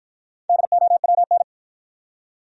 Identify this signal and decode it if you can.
Morse code, heard as a keyed tone, about 52 wpm, 700 Hz. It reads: DOWN